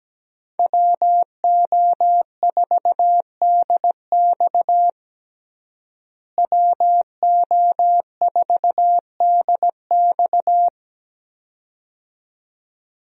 Morse code, heard as a keyed tone, about 17 wpm, 700 Hz. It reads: WO4DX WO4DX
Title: USA Call Signs - 5 Characters - Repeated 2x - 17wpm